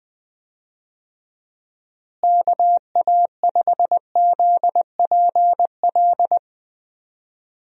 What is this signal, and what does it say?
Morse code, heard as a keyed tone, about 20 wpm, 700 Hz. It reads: KA5ZPL
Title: USA Call Signs - 6 Characters - Repeated 2x - 20wpm